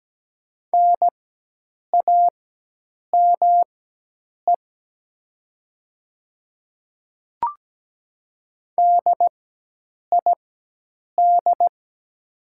Morse code, heard as a keyed tone, about 17 wpm, spaced as for 8 wpm, 700 Hz. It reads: NAME DID
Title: 4x Character Spacing - Top 200 Words - 17wpm